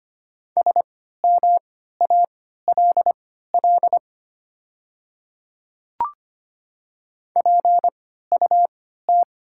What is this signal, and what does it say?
Morse code, heard as a keyed tone, about 25 wpm, spaced as for 14 wpm, 700 Hz. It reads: SMALL PUT
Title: 3x Character Spacing - Top 200 Words - 25wpm